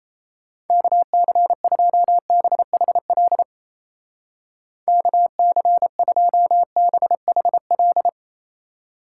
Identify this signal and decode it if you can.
Morse code, heard as a keyed tone, about 36 wpm, 700 Hz. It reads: KC2BHL KC2BHL